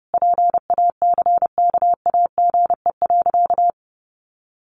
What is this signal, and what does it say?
Morse code, heard as a keyed tone, about 30 wpm, 700 Hz. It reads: PACKAGE.
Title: Additional Easy Phrases 30wpm